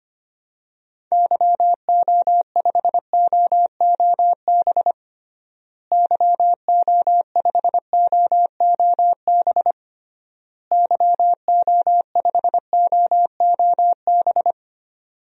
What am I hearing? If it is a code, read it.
Morse code, heard as a keyed tone, about 25 wpm, 700 Hz. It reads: YO5OOB YO5OOB YO5OOB
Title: DXCC-100 Call Signs - 6 Characters - Repeated 3x - 25wpm